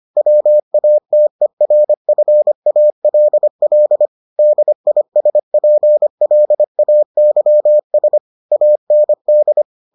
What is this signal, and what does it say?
Morse code, heard as a keyed tone, about 25 wpm, 600 Hz. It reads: WATERFALL DISPLAYS AND